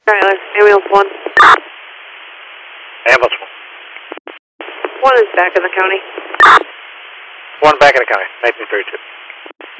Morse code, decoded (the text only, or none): none